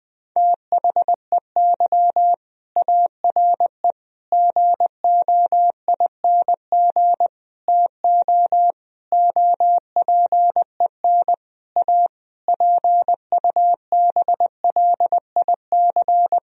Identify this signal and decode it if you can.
Morse code, heard as a keyed tone, about 20 wpm, 700 Hz. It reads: THEY ARE GOING TO OPEN A PUBLIC
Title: Sentences from Top 1000 Words - Review Part 3 - 20wpm